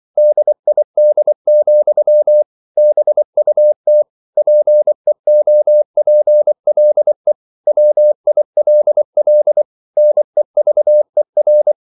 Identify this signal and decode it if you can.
Morse code, heard as a keyed tone, about 24 wpm, 600 Hz. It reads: DID, BUT PEOPLE WILL NEVER
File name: CW / Practice / EnglishQuotations0224